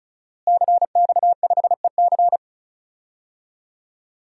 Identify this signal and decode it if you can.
Morse code, heard as a keyed tone, about 36 wpm, 700 Hz. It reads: CX5EC
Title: DXCC-100 Call Signs - 5 Characters - Repeated 3x - 35wpm